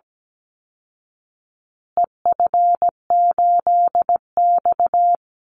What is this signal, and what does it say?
Morse code, heard as a keyed tone, about 17 wpm, 700 Hz. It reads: EF8X
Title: Call Signs Repeated 2x - 17wpm